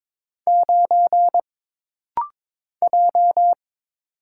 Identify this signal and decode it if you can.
Morse code, heard as a keyed tone, about 22 wpm, 700 Hz. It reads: 9 J
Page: This rapid-fire timing is geared to push your ICR (Instant Character Recognition) skills.